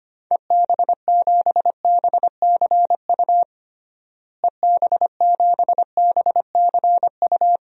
Morse code, heard as a keyed tone, about 25 wpm, 700 Hz. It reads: EB7BCU EB7BCU
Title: DXCC-100 Call Signs - 6 Characters - Repeated 3x - 25wpm